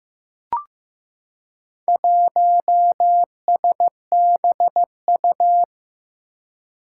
Morse code, heard as a keyed tone, about 15 wpm, 700 Hz. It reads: 1SBU
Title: Four Challenging Letter-Number Combinations 15wpm